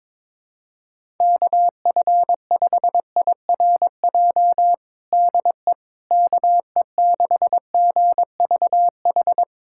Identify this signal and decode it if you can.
Morse code, heard as a keyed tone, about 22 wpm, 700 Hz. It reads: KF5IRJ DE KE6GVH
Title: USA Callsign DE Callsign - 6 Characters - 22wpm